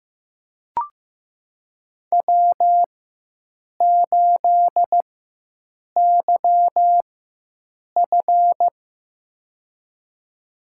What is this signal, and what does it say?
Morse code, heard as a keyed tone, about 15 wpm, spaced as for 7 wpm, 700 Hz. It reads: W8YF
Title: USA Call Signs - 4 Characters - 4x Character Spacing - 15wpm